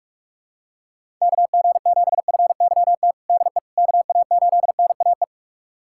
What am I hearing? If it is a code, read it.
Morse code, heard as a keyed tone, about 41 wpm, 700 Hz. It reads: KG7FYT DE KA8NAE
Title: USA Callsign DE Callsign - 6 Characters - 45wpm